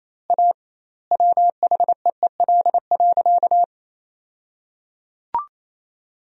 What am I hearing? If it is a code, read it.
Morse code, heard as a keyed tone, about 27 wpm, 700 Hz. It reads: A WHEEL.
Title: 2x Word-Spacing - Sentences from Top 500 Words - 28wpm